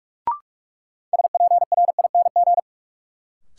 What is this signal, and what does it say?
Morse code, heard as a keyed tone, about 43 wpm, 700 Hz. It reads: SPRING
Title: Sentences from Top 1200 Words - New Word Drill - 45wpm